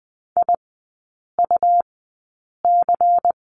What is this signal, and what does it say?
Morse code, heard as a keyed tone, about 20 wpm, spaced as for 8 wpm, 700 Hz. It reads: IUC
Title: 100 Lesson 17e - Letter u - Three Characters 20wpm